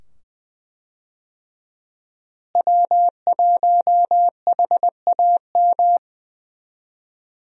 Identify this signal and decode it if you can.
Morse code, heard as a keyed tone, about 20 wpm, 700 Hz. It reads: W1HAM
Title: USA Call Signs - 5 Characters - Repeated 2x - 20wpm